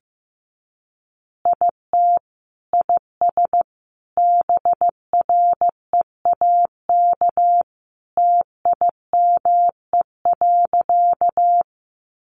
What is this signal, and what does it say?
Morse code, heard as a keyed tone, about 15 wpm, 700 Hz. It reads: IT IS BREAK TIME.
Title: Easy Phrases 15wpm